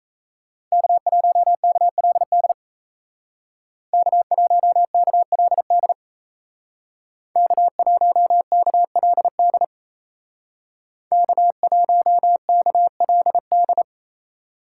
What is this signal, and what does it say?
Morse code, heard as a keyed tone, about 33 wpm, 700 Hz. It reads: K1KLD K1KLD K1KLD K1KLD
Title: Speed Racing - USA 1x3 Call Signs 28wpm